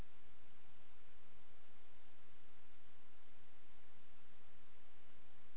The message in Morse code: none

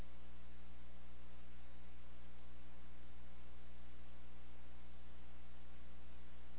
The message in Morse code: none

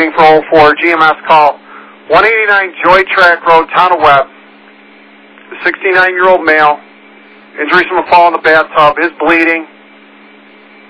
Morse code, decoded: none

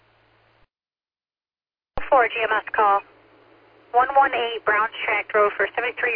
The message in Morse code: none

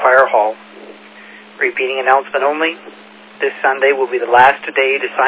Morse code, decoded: none

